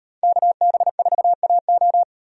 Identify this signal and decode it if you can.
Morse code, heard as a keyed tone, about 37 wpm, 700 Hz. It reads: KB4AO